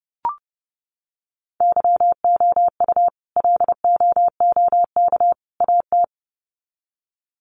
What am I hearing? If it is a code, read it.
Morse code, heard as a keyed tone, about 30 wpm, 700 Hz. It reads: YOU LOOK AT
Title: Sets of 3 Words 30wpm